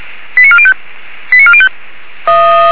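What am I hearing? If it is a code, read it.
Five-tone selective call (ZVEI), heard as quick runs of five tones, tones of about 70 ms. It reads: 89405 89405
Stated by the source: Five-tone ZVEI selective call received by SDR, from the Feuerwehr Orlamünde (feuerwehr-orlamuende.org) archive.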